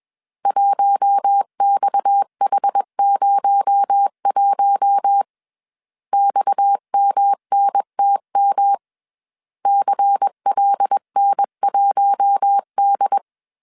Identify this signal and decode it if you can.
Morse code, heard as a keyed tone, about 21 wpm, 800 Hz. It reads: 1XH01 XMNTM CLN1D
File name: CW / DET / 1 6 1500C S A 20wpm 2048